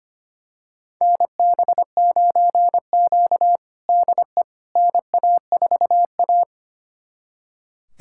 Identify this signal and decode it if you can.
Morse code, heard as a keyed tone, about 25 wpm, 700 Hz. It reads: NB9Q DE NA4A
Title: USA Callsign DE Callsign - 4 Characters - 25wpm